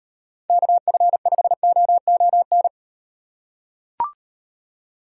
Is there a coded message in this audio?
Yes: Morse code from 0.5 to 2.7 s, heard as a keyed tone, about 38 wpm, 700 Hz. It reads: KF5OON